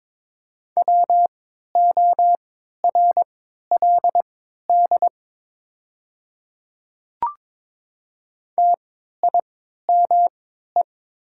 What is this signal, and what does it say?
Morse code, heard as a keyed tone, about 22 wpm, spaced as for 12 wpm, 690 Hz. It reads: WORLD TIME
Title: 3x Character Spacing - Top 200 Words - 22wpm